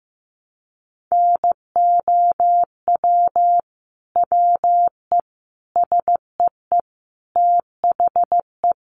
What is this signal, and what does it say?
Morse code, heard as a keyed tone, about 15 wpm, 700 Hz. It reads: NOW WE SEE THE
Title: Sentences from Top 200 Words 15wpm